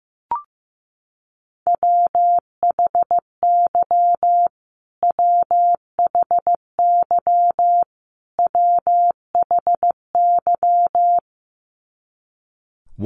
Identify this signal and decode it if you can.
Morse code, heard as a keyed tone, about 15 wpm, 700 Hz. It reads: WHY WHY WHY
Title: Top 500 Words Repeated 15wpm